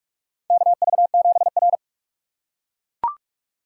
Morse code, heard as a keyed tone, about 42 wpm, 700 Hz. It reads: KV7R
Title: USA Call Signs - 4x Character Spacing - 45wpm